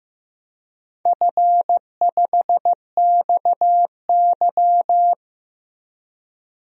Morse code, heard as a keyed tone, about 15 wpm, 700 Hz. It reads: F5XY
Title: DXCC-100 Call Signs - 4 Characters - Repeated 3x - 15wpm